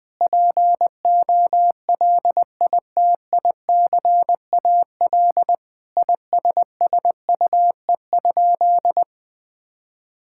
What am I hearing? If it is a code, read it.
Morse code, heard as a keyed tone, about 21 wpm, 700 Hz. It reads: POLITICAL ISSUE?
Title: Sentences from Top 1000 Words - Review Part 6 - 20wpm